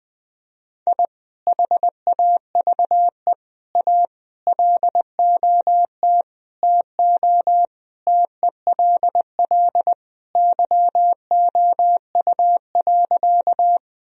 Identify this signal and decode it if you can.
Morse code, heard as a keyed tone, about 19 wpm, 700 Hz. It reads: I HAVE A LOT TO TELL YOU.
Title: Sentences from Top 1000 Words - Review Part 3 - 20wpm